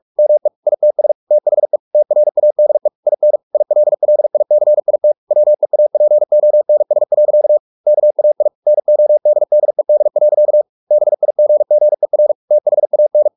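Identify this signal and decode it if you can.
Morse code, heard as a keyed tone, about 44 wpm, 600 Hz. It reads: ME ITS THE TRADE IN ILLICIT WEAPONS. KAI NODDED. BIGGER THAN
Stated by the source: Morse practice at 45 wpm, the novel Never by Ken Follett in English, from CW med Gnister.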